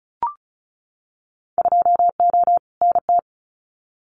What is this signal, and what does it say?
Morse code, heard as a keyed tone, about 36 wpm, 700 Hz. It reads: 2O NT
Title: Winter Field Day Exchanges Encore 35wpm